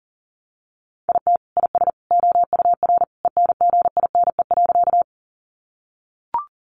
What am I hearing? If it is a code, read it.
Morse code, heard as a keyed tone, about 37 wpm, 700 Hz. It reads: IT IS OUR ENGINE.
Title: Sentences from Top 1500 Words 40wpm